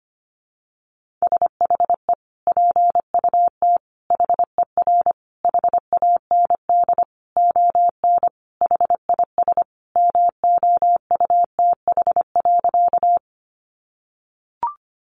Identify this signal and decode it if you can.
Morse code, heard as a keyed tone, about 25 wpm, 700 Hz. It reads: SHE PUT HER HAND ON HIS MOUTH.
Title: Sentences from Top 1500 Words 25wpm